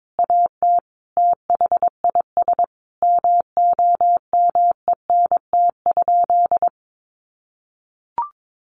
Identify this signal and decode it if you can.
Morse code, heard as a keyed tone, about 22 wpm, 700 Hz. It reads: AT THIS MOMENT?
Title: Sentences from Top 1000 Words 22wpm